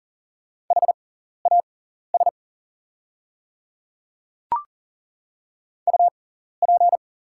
Morse code, heard as a keyed tone, about 40 wpm, spaced as for 14 wpm, 700 Hz. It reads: HAS UP
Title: 6x Character Spacing - Top 300 Words - 40wpm